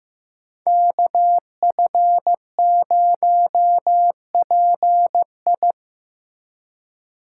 Morse code, heard as a keyed tone, about 15 wpm, 700 Hz. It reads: KF0PI
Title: USA Call Signs - 5 Characters - 4x Character Spacing - 15wpm